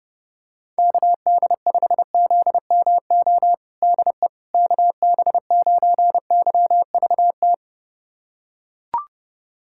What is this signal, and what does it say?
Morse code, heard as a keyed tone, about 30 wpm, 700 Hz. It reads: KD5ZMO DE KB9YVT
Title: USA Callsign DE Callsign - 6 Characters - 30wpm